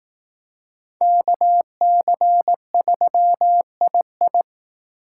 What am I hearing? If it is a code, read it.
Morse code, heard as a keyed tone, about 18 wpm, 700 Hz. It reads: KC3II
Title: Speed Racing - USA 2x2 Call Signs 15wpm